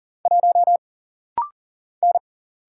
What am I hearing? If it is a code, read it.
Morse code, heard as a keyed tone, about 40 wpm, 700 Hz. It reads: 1 N
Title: Single Letter-Number Alt - Rapid-Fire 40wpm